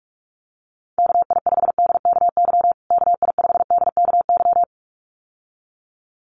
Morse code, heard as a keyed tone, about 44 wpm, 700 Hz. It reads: KI5DKY KI5DKY
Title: USA Call Signs Repeated 2x 45wpm